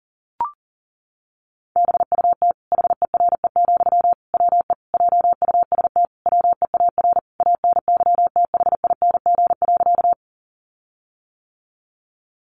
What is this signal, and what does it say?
Morse code, heard as a keyed tone, about 40 wpm, 700 Hz. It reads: BUT HERE, WE JUST WEAR ANYTHING.
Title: Sentences from Top 1500 Words 40wpm